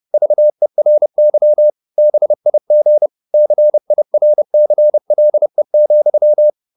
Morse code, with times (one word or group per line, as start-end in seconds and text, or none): 0.1-1.7 VERY
2.0-3.1 BIG
3.3-6.5 CIRCLE,